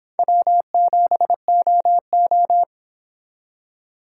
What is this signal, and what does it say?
Morse code, heard as a keyed tone, about 26 wpm, 700 Hz. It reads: W7OO